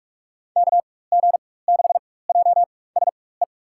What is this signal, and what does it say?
Morse code, heard as a keyed tone, about 44 wpm, spaced as for 21 wpm, 690 Hz. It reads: KG6JSE
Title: USA Call Signs - 4x Character Spacing - 45wpm